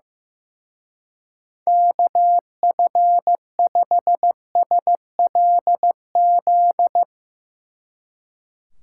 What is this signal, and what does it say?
Morse code, heard as a keyed tone, about 15 wpm, 700 Hz. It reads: KF5SLZ
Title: Speed Racing - USA 2x3 Call Signs 15wpm